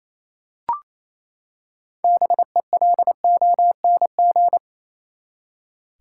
Morse code, heard as a keyed tone, about 28 wpm, 700 Hz. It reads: BELONG
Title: Six Letter Words - 28wpm